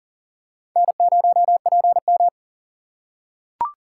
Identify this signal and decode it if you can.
Morse code, heard as a keyed tone, about 40 wpm, 700 Hz. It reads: N0PM
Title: USA Call Signs - 4 Characters - 4x Character Spacing - 40wpm